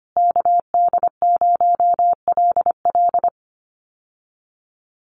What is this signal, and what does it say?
Morse code, heard as a keyed tone, about 25 wpm, 700 Hz. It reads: KD0LL
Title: USA Call Signs Repeated 3x 25wpm